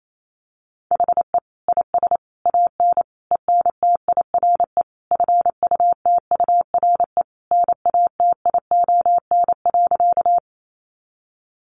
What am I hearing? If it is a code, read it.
Morse code, heard as a keyed tone, about 28 wpm, 700 Hz. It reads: HE IS AN ENTIRE FUTURE NATION.
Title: A Sound Of Thunder 28wpm